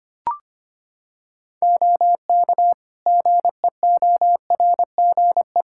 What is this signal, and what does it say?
Morse code, heard as a keyed tone, about 25 wpm, 700 Hz. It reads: OK GEORGE
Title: K1USN SST QSO Practice 25wpm